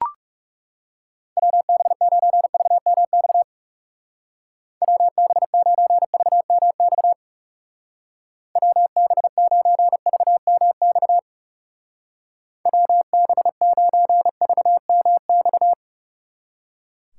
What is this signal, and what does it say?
Morse code, heard as a keyed tone, about 36 wpm, 700 Hz. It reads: WB9VMX WB9VMX WB9VMX WB9VMX
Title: Speed Racing - USA 2x3 Call Signs 30wpm